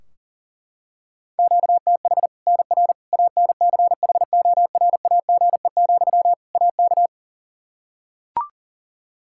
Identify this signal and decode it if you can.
Morse code, heard as a keyed tone, about 40 wpm, 700 Hz. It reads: QTH NR ANCHORAGE, AK
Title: K1USN SST QSO Practice 40wpm